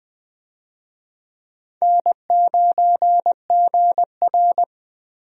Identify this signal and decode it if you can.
Morse code, heard as a keyed tone, about 21 wpm, 700 Hz. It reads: N9GR
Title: USA Call Signs - 4 Characters - Repeated 3x - 20wpm